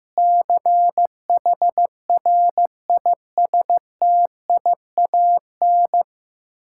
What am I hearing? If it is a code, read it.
Morse code, heard as a keyed tone, about 16 wpm, 700 Hz. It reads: CHRISTIAN
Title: K1USN SST QSO Practice 15wpm